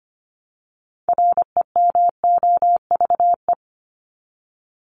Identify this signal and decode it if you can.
Morse code, heard as a keyed tone, about 25 wpm, 700 Hz. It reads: REMOVE
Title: Top 751-1000 Words 25wpm